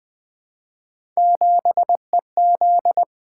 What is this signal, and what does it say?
Morse code, heard as a keyed tone, about 20 wpm, 700 Hz. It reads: 7EZ
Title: Three Letter-Number Combinations - 20wpm